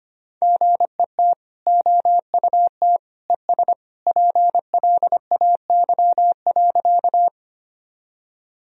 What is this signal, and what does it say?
Morse code, heard as a keyed tone, about 25 wpm, 700 Hz. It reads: GET OUT ES PLAY.
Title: Sentences from Top 200 Words and Abbreviations 25wpm